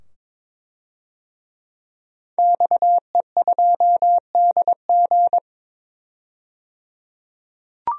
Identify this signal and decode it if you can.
Morse code, heard as a keyed tone, about 22 wpm, 700 Hz. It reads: XE2DG
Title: DXCC-100 Call Signs - 5 Characters - 22wpm